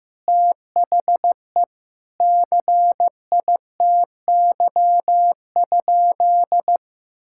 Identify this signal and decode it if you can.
Morse code, heard as a keyed tone, about 15 wpm, 700 Hz. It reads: THE CITY?
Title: Sentences from Top 1600 Words 15wpm